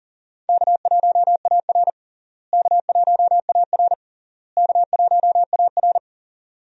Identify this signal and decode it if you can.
Morse code, heard as a keyed tone, about 40 wpm, 700 Hz. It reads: K1AR K1AR K1AR
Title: USA Call Signs - 4 Characters - Repeated 3x - 40wpm